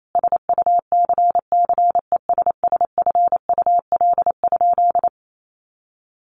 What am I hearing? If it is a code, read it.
Morse code, heard as a keyed tone, about 28 wpm, 700 Hz. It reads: SUCCESSFUL?
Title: A Sound Of Thunder 28wpm